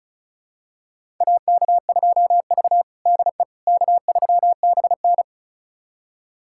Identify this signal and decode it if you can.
Morse code, heard as a keyed tone, about 36 wpm, 700 Hz. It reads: AK2V DE K3BN
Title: USA Callsign DE Callsign - 4 Characters - 35wpm